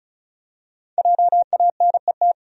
Morse code, heard as a keyed tone, about 34 wpm, 700 Hz. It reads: JANET
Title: USA Common Names - 35wpm